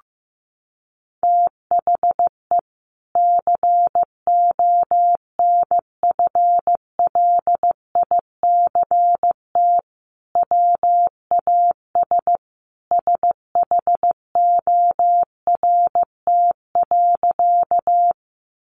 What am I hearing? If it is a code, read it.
Morse code, heard as a keyed tone, about 15 wpm, 700 Hz. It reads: THE CONFLICT WAS SHORT.
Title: Sentences from Top 1300 Words 15wpm